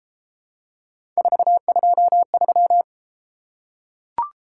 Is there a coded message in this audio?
Yes: Morse code from 1.2 to 2.8 s, heard as a keyed tone, about 32 wpm, 700 Hz. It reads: 423